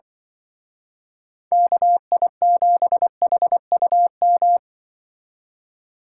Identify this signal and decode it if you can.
Morse code, heard as a keyed tone, about 28 wpm, 700 Hz. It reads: KI7HUM